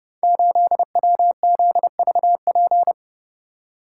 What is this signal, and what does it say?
Morse code, heard as a keyed tone, about 31 wpm, 700 Hz. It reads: 8WZVP
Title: Code Groups - 5 Letter:Numbers - 30wpm